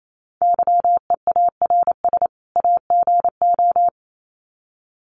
Morse code, heard as a keyed tone, about 28 wpm, 700 Hz. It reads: YEARS AGO
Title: Sets of 2 Words 28wpm